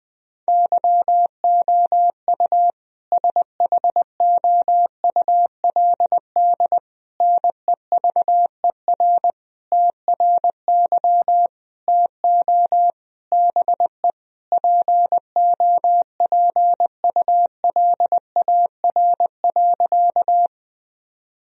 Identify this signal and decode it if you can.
Morse code, heard as a keyed tone, about 20 wpm, 700 Hz. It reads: YOU SHOULD NEVER TRY TO BE POPULAR.
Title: Sentences from Top 1000 Words - Review Part 3 - 20wpm